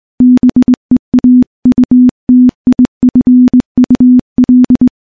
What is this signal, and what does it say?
Morse code, heard as a keyed tone, about 20 wpm, 250 Hz. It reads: BEAUTIFUL
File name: morse / morse-training-20WPM